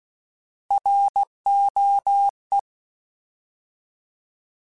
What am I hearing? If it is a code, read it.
Morse code, heard as a keyed tone, about 16 wpm, 800 Hz. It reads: ROE
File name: Alphabetical / Alphabetical mp3s / Rs @15